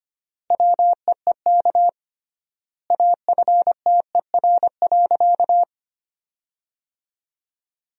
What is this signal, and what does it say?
Morse code, heard as a keyed tone, about 25 wpm, 700 Hz. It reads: WEEK AFTER.
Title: 3x Word-Spacing - Sentences from Top 500 Words - 25wpm